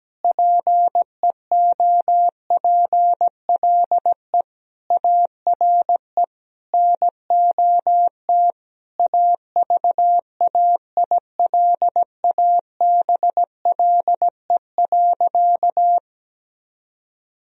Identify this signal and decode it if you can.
Morse code, heard as a keyed tone, about 17 wpm, 700 Hz. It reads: PEOPLE ARE NOT AVAILABLE.
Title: Sentences from Top 1000 Words - Review Part 3 - 17wpm